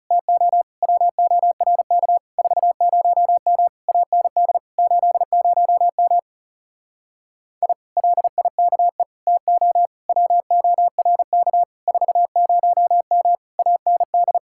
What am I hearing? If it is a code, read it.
Morse code, heard as a keyed tone, about 40 wpm, 700 Hz. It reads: TO WORK 40M AND 80M I LIKE TO WORK 40M AND